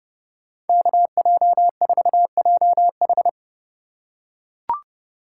Morse code, heard as a keyed tone, about 31 wpm, 700 Hz. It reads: KJ4JH